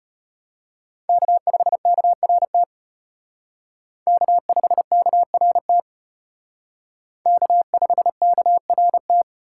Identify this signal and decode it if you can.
Morse code, heard as a keyed tone, about 32 wpm, 700 Hz. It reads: K5KRT K5KRT K5KRT